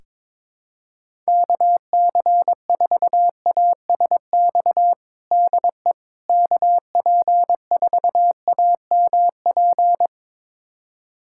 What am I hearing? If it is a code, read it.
Morse code, heard as a keyed tone, about 22 wpm, 700 Hz. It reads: KC4ASX DE KP4AMP
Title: USA Callsign DE Callsign 22wpm